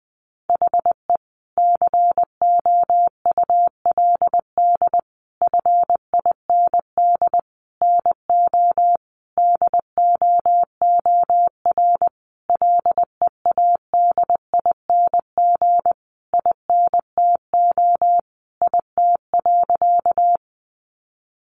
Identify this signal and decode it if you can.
Morse code, heard as a keyed tone, about 19 wpm, 700 Hz. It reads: HE COULD FIND NO DOOR LEADING INTO IT.
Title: Sentences from Top 1100 Words 20wpm